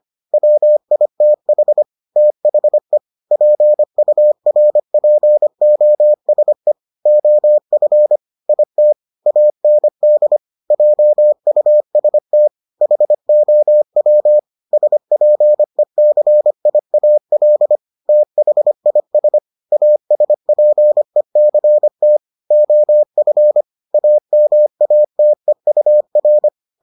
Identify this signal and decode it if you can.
Morse code, heard as a keyed tone, about 25 wpm, 600 Hz. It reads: WITH THE PURPOSE OF IT AND JUST HOW SPECIAL THIS ASPECT OF AMATEUR